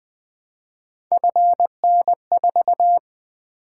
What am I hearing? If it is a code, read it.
Morse code, heard as a keyed tone, about 20 wpm, 700 Hz. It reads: FN4
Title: Three Letter-Number Combinations - 20wpm